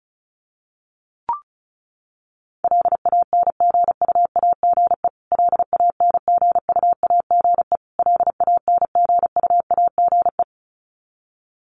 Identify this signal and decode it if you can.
Morse code, heard as a keyed tone, about 34 wpm, 700 Hz. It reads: LANGUAGE LANGUAGE LANGUAGE
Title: Top 500 Words Repeated 35wpm